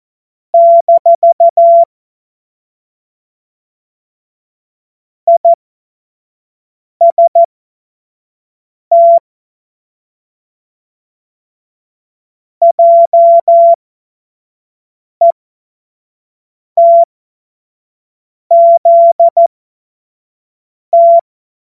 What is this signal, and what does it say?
Morse code, heard as a keyed tone, about 14 wpm, spaced as for 5 wpm, 680 Hz. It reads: - IST JETZT
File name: Morsekurs / Witze / Witz 27. Juni 2023 25 F70